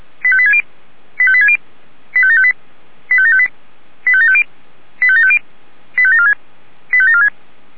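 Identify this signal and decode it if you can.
Five-tone selective call (ZVEI), heard as quick runs of five tones, tones of about 70 ms. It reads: 85860 85860 85758 85758 85750 85750 85746 85746
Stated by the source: ZVEI five-tone call received by SDR.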